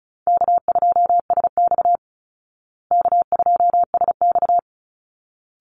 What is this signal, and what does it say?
Morse code, heard as a keyed tone, about 34 wpm, 700 Hz. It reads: K2SX K2SX
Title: Call Signs Repeated 3x - 35wpm